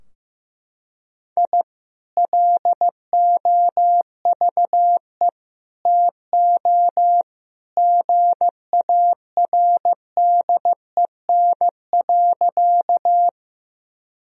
Morse code, heard as a keyed tone, about 15 wpm, 700 Hz. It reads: I LOVE TO GARDEN.
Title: Sentences from Top 1000 Words - Review Part 4 - 15wpm